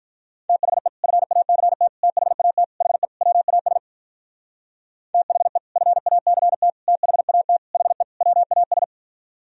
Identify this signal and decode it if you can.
Morse code, heard as a keyed tone, about 46 wpm, 700 Hz. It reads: THE FACT THAT HE WAS THE FACT THAT HE WAS